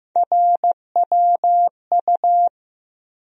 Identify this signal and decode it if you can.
Morse code, heard as a keyed tone, about 15 wpm, 700 Hz. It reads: RWU
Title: USA Call Sign Postfixes - 3 Letters 15wpm